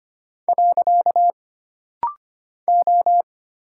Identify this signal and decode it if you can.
Morse code, heard as a keyed tone, about 25 wpm, 700 Hz. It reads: . O
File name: Single Character (Letter:Number:Punctuation:Prosign) - Rapid-Fire - 25wpm